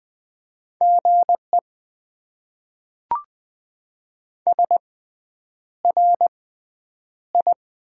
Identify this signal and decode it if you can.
Morse code, heard as a keyed tone, about 20 wpm, spaced as for 9 wpm, 700 Hz. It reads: GE SRI
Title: Top 90 QSO Elements - 6x Character Spacing - 20wpm